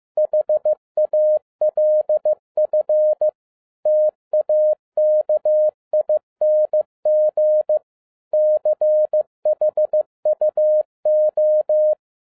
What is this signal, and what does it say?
Morse code, heard as a keyed tone, about 15 wpm, 610 Hz. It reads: HALF TAKING CHUO
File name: 2011-07-mp3 / n47 JOHN K3TN 15wpm